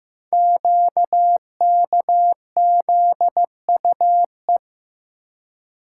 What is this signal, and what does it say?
Morse code, heard as a keyed tone, about 15 wpm, 700 Hz. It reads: QKZUE
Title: Code Groups - 5 Letters - 15wpm